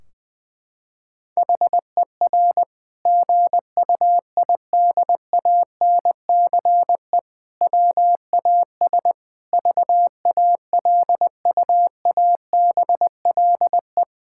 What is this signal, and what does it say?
Morse code, heard as a keyed tone, about 20 wpm, 700 Hz. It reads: HER GUIDANCE WAS VALUABLE
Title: Sentences from Top 1600 Words 20wpm